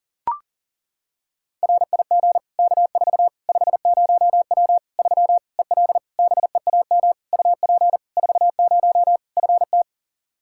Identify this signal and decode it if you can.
Morse code, heard as a keyed tone, about 40 wpm, 700 Hz. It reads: RIG K4 50W 3 EL BEAM UP 40 FT
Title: Practice QSO - Exchange 2 - Info - 40wpm